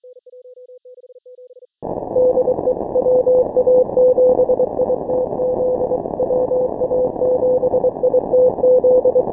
Morse code, heard as a keyed tone, about 23 wpm, 530 Hz, with noise. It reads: DE WA7ITZ WA7ITZ
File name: final / W1UJ-168-140-MorseRunner